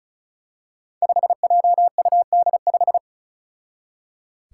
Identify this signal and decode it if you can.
Morse code, heard as a keyed tone, about 34 wpm, 700 Hz. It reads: 5JUD5
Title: Five Challenging Letter-Number Combinations 35wpm